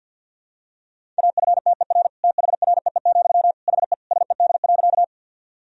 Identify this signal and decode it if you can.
Morse code, heard as a keyed tone, about 49 wpm, 700 Hz. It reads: AFTER THREE, HE SED.
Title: Sentences from Top 200 Words and Abbreviations 50wpm